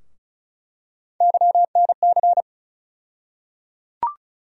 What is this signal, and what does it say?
Morse code, heard as a keyed tone, about 36 wpm, 700 Hz. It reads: YNC
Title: USA Call Sign Postfixes - 3 Letters 35wpm